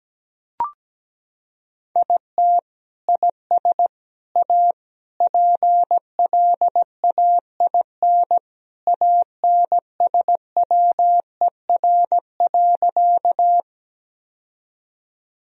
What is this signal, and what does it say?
Morse code, heard as a keyed tone, about 17 wpm, 700 Hz. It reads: IT IS A PLAIN ANSWER.
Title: Sentences from Top 1000 Words - Review Part 2 - 17wpm